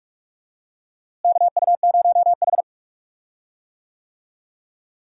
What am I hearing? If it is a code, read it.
Morse code, heard as a keyed tone, about 46 wpm, 700 Hz. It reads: KU0H
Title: USA Call Signs - 4 Characters - Repeated 3x - 45wpm